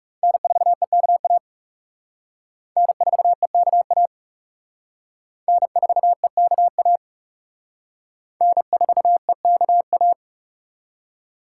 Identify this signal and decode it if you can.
Morse code, heard as a keyed tone, about 37 wpm, 700 Hz. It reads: N4EKA N4EKA N4EKA N4EKA
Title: Speed Racing - USA 1x3 Call Signs 30wpm